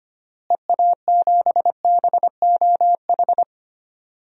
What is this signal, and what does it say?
Morse code, heard as a keyed tone, about 25 wpm, 700 Hz. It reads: EA7BOH
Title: DXCC-100 Call Signs - 6 Characters - Repeated 3x - 25wpm